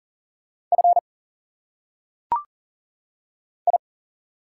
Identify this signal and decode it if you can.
Morse code, heard as a keyed tone, about 40 wpm, 700 Hz. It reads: F I
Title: Single Letters - Phonetic Pronunciation 40wpm